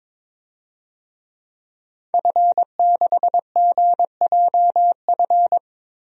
Morse code, heard as a keyed tone, about 22 wpm, 700 Hz. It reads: F6GJF
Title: DXCC-100 Call Signs - 5 Characters - Repeated 3x - 22wpm